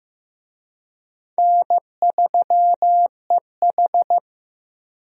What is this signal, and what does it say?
Morse code, heard as a keyed tone, about 16 wpm, 700 Hz. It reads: N3EH
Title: USA Call Signs - 4 Characters - Repeated 3x - 15wpm